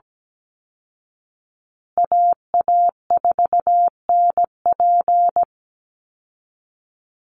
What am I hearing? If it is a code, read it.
Morse code, heard as a keyed tone, about 17 wpm, 700 Hz. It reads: AA4NP
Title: Call Signs Repeated 2x - 17wpm